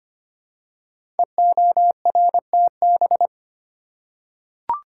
Morse code, heard as a keyed tone, about 25 wpm, 700 Hz. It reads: EORTB
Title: Code Groups - 5 Letter:Numbers - 25wpm